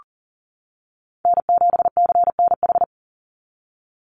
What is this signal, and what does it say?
Morse code, heard as a keyed tone, about 40 wpm, 700 Hz. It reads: N7CNH
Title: NA SOTA Chaser Callsigns 40wpm